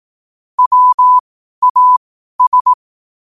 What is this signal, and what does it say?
Morse code, heard as a keyed tone, about 18 wpm, spaced as for 12 wpm, 1000 Hz. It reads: WAS
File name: qotdfiles / 12wpm